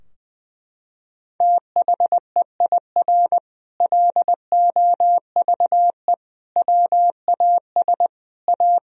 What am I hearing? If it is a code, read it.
Morse code, heard as a keyed tone, about 20 wpm, 700 Hz. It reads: THEIR LOVE WAS A
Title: Sentences from Top 1000 Words - Review Part 4 - 20wpm